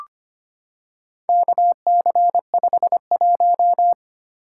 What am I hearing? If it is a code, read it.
Morse code, heard as a keyed tone, about 25 wpm, 700 Hz. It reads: KC51
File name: practice / 25wpm / Four Letter-Number Combinations - 25wpm